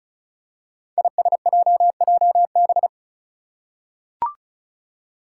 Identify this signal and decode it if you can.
Morse code, heard as a keyed tone, about 36 wpm, 700 Hz. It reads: ISJJB
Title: Five Challenging Letter-Number Combinations 35wpm